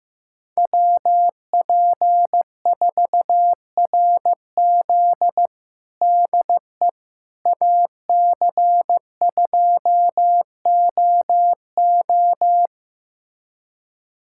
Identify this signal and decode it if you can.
Morse code, heard as a keyed tone, about 15 wpm, 700 Hz. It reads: WP4RZ DE AC2OO
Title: USA Callsign DE Callsign - 5 Characters - 15wpm